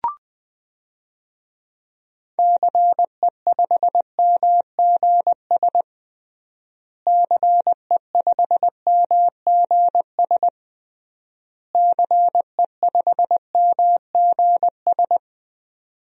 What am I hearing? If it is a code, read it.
Morse code, heard as a keyed tone, about 20 wpm, 700 Hz. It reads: CE5MGS CE5MGS CE5MGS
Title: DXCC-100 Call Signs - 6 Characters - Repeated 3x - 20wpm